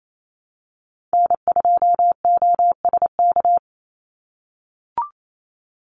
Morse code, heard as a keyed tone, about 28 wpm, 700 Hz. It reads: N2OSK
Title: USA Call Signs 28wpm